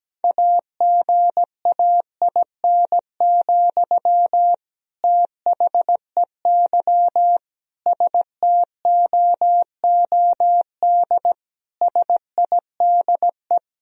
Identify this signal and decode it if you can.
Morse code, heard as a keyed tone, about 17 wpm, 700 Hz. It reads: AGAIN, THEY STOOD SIDE
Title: Sentences from Top 1000 Words - Review Part 2 - 17wpm